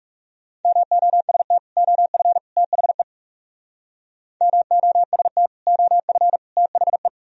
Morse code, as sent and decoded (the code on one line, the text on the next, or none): -- --- ... - / --- ..-. / - .... / -- --- ... - / --- ..-. / - .... .
MOST OF THE MOST OF THE